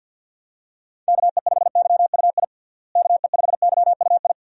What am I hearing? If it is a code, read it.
Morse code, heard as a keyed tone, about 51 wpm, 700 Hz. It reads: KE5YUI KE5YUI
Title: USA Call Signs - 6 Characters - Repeated 2x - 50wpm